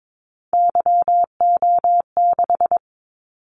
Morse code, heard as a keyed tone, about 22 wpm, 700 Hz. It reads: YO6
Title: Callsign Prefixes for Top 25 DXCC - 22wpm